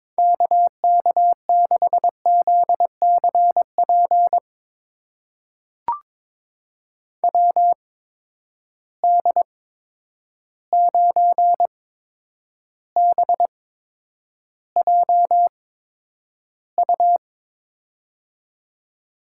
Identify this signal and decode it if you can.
Morse code, heard as a keyed tone, about 22 wpm, spaced as for 9 wpm, 700 Hz. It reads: KK6ZCP WD9BJU
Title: USA Call Signs - 8x Character Spacing - 22wpm